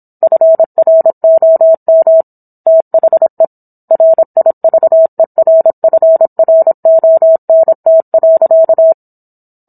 Morse code, heard as a keyed tone, about 27 wpm, 650 Hz. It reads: FROM THE RIVERFRONT.
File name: Book One / 17-Chapter 17 - 26 WPM